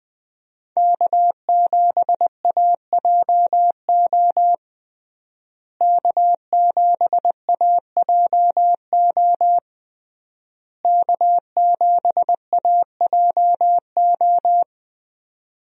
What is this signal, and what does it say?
Morse code, heard as a keyed tone, about 20 wpm, 700 Hz. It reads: K7AJO K7AJO K7AJO
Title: USA Call Signs - 5 Characters - Repeated 3x - 20wpm